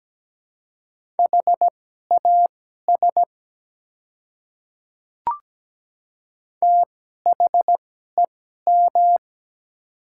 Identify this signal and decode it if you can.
Morse code, heard as a keyed tone, about 17 wpm, spaced as for 12 wpm, 700 Hz. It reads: HAS THEM
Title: 2x Character Spacing - Top 100 Words - 17wpm